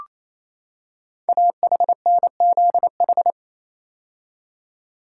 Morse code, heard as a keyed tone, about 27 wpm, 700 Hz. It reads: AHNZH
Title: Code Groups - 5 Letters - 28wpm